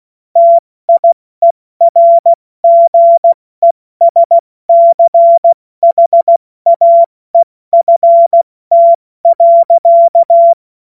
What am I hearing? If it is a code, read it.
Morse code, heard as a keyed tone, about 16 wpm, 680 Hz. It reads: TIERGESCHAEFT.
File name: Morsekurs / Witze / Witz 28. September 2021, 70 F80